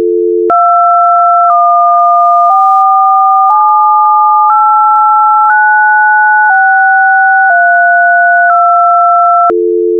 DTMF, heard as telephone touch tones, tones of about 1010 ms. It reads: 214789632